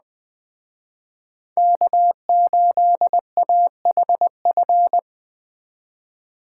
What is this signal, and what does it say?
Morse code, heard as a keyed tone, about 20 wpm, 700 Hz. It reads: K8AHF
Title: Speed Racing - USA Call Signs 17wpm